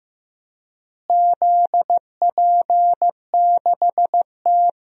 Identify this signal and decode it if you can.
Morse code, heard as a keyed tone, about 15 wpm, 700 Hz. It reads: ZP6T
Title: DXCC-100 Call Signs - 4 Characters - Repeated 3x - 15wpm